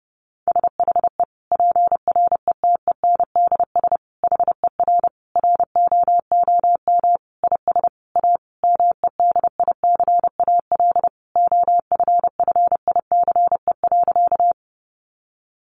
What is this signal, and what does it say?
Morse code, heard as a keyed tone, about 30 wpm, 700 Hz. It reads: SHE PRETENDS HER ROOM IS A MEDICAL OFFICE.
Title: Additional Easy Phrases 30wpm